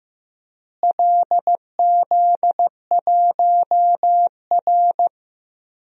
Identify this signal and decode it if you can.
Morse code, heard as a keyed tone, about 16 wpm, 700 Hz. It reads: LZ1R
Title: DXCC-100 Call Signs - 4 Characters - Repeated 3x - 15wpm